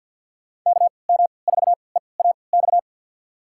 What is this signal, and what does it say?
Morse code, heard as a keyed tone, about 54 wpm, spaced as for 29 wpm, 700 Hz. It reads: KM4EAX
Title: USA Call Signs - 6 Characters - 3x Character Spacing - 50wpm